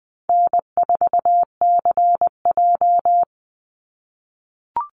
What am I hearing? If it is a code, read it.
Morse code, heard as a keyed tone, about 20 wpm, 700 Hz. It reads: N4CJ
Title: Easy Call Signs - 20wpm